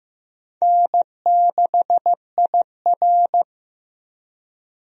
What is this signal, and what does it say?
Morse code, heard as a keyed tone, about 15 wpm, 700 Hz. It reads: N6IR
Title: USA Call Signs - 4 Characters - Repeated 2x - 15wpm